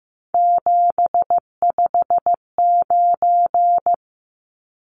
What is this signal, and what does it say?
Morse code, heard as a keyed tone, about 15 wpm, 700 Hz. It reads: 759
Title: Numbers - 3-Digit 15wpm